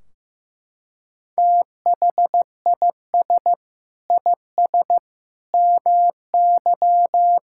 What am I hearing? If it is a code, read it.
Morse code, heard as a keyed tone, about 15 wpm, 700 Hz. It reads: THIS IS MY
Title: Sentences from Top 1600 Words 15wpm